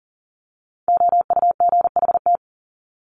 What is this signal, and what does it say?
Morse code, heard as a keyed tone, about 40 wpm, 700 Hz. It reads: OUGHT
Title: Top 1751-2000 Words 40wpm